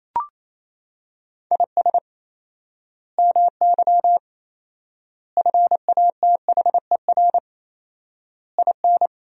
Morse code, heard as a keyed tone, about 28 wpm, 700 Hz. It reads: IS MY FATHER IN
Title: 4x Word-Spacing - Sentences from Top 500 Words - 28wpm